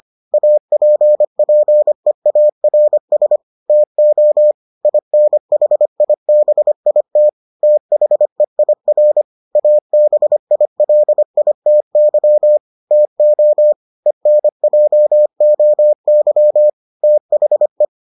Morse code, heard as a keyed tone, about 25 wpm, 600 Hz. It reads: APPEARS TO INHIBIT THEIR ABILITY TO ENJOY THE